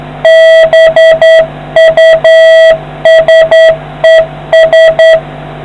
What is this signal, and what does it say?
Morse code, heard as a keyed tone, about 10 wpm, 650 Hz, with noise. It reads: BUSES